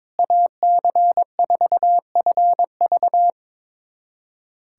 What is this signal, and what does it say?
Morse code, heard as a keyed tone, about 21 wpm, 700 Hz. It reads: AC4FV
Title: USA Callsign DE Callsign - 5 Characters - 22wpm